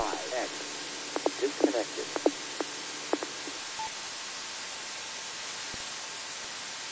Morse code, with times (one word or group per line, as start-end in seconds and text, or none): none